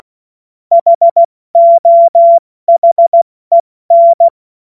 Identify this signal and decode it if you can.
Morse code, heard as a keyed tone, about 16 wpm, 680 Hz. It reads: HOHEN
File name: Morsekurs / Witze / Witz 30. Dezember 2025 70 F80